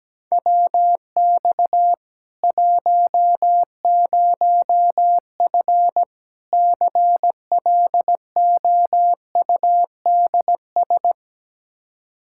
Morse code, heard as a keyed tone, about 17 wpm, 700 Hz. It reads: WX 10F CLOUDS
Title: Practice QSO - Exchange 2 - Info - 17wpm